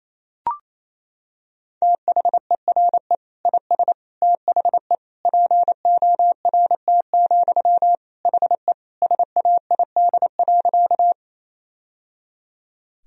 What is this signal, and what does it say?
Morse code, heard as a keyed tone, about 28 wpm, 700 Hz. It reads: THERE IS THE PORT, HE SAID.
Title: Sentences from Top 1000 Words - Review Part 1 - 28wpm